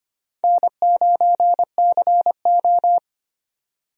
Morse code, heard as a keyed tone, about 25 wpm, 700 Hz. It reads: N9CO
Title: USA Call Signs - 4 Characters - 25wpm